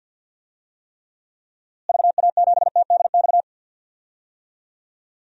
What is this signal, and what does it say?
Morse code, heard as a keyed tone, about 47 wpm, 700 Hz. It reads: VA7TDX
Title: DXCC-100 Call Signs - 6 Characters - Repeated 2x - 50wpm